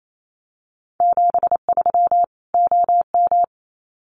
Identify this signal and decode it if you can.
Morse code, heard as a keyed tone, about 28 wpm, 700 Hz. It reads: 73 OM
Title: Common QSO Phrases 28wpm